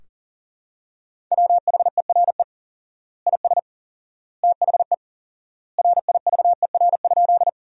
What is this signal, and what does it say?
Morse code, heard as a keyed tone, about 40 wpm, 700 Hz. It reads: WHERE IS THE RIVER?
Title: 4x Word-Spacing - Sentences from Top 300 Words - 40wpm